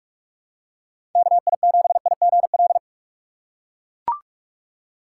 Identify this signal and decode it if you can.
Morse code, heard as a keyed tone, about 44 wpm, 700 Hz. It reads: KI7IGL
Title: USA Call Signs - 6 Characters - 6x Character Spacing - 45wpm